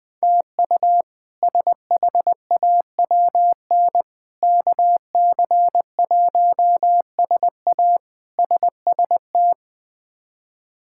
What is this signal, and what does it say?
Morse code, heard as a keyed tone, about 21 wpm, 700 Hz. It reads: TU SHAWN KC1SA SST
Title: K1USN SST QSO Practice - 20wpm